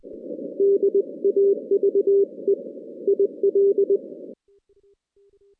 Morse code, heard as a keyed tone, about 20 wpm, 400 Hz. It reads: DAVE IL
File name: 2016 JAN CW NAQP / 2016 Jan naqp k5ot - 0-10